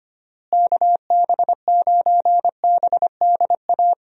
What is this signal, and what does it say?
Morse code, heard as a keyed tone, about 25 wpm, 700 Hz. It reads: KB9BDA
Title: USA Call Signs - 6 Characters - Repeated 2x - 25wpm